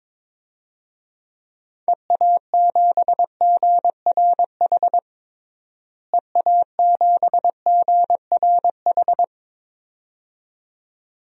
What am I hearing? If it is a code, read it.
Morse code, heard as a keyed tone, about 22 wpm, 700 Hz. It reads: EA7GRH EA7GRH
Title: DXCC-100 Call Signs - 6 Characters - Repeated 2x - 22wpm